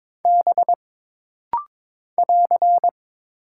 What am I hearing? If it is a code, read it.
Morse code, heard as a keyed tone, about 22 wpm, 700 Hz. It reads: B <AR>
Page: This rapid-fire timing is geared to push your ICR (Instant Character Recognition) skills.